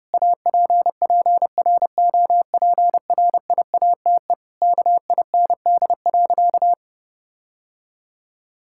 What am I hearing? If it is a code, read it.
Morse code, heard as a keyed tone, about 30 wpm, 700 Hz. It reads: APPROPRIATE KIND.
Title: Sentences from Top 1000 Words - Review Part 5 - 30wpm